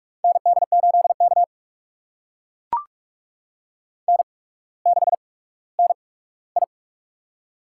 Morse code, heard as a keyed tone, about 44 wpm, spaced as for 19 wpm, 700 Hz. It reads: ND8K N6NI
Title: USA Call Signs - 4 Characters - 8x Character Spacing - 45wpm